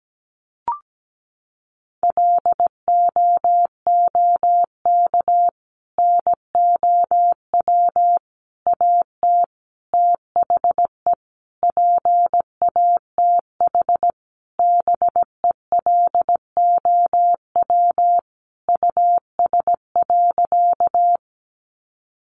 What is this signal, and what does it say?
Morse code, heard as a keyed tone, about 17 wpm, 700 Hz. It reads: LOOK NOW AT THE PATH BELOW US.
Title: Sentences from Top 1400 Words 17wpm